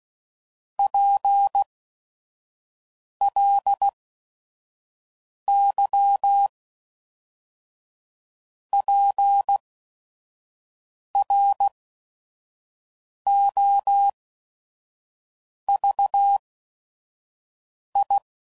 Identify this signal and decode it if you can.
Morse code, heard as a keyed tone, about 16 wpm, spaced as for 5 wpm, 800 Hz. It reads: PLY PROVI
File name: CW Code Groups @5WPM